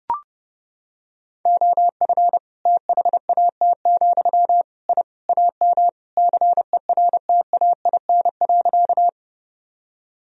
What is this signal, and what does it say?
Morse code, heard as a keyed tone, about 30 wpm, 700 Hz. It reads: OF THAT, I AM CERTAIN.
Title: Sentences from Top 1000 Words - Review Part 2 - 30wpm